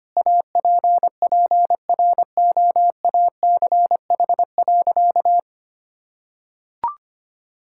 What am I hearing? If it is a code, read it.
Morse code, heard as a keyed tone, about 25 wpm, 700 Hz. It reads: APPROACH.
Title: Sentences from Top 1000 Words - Review Part 6 - 25wpm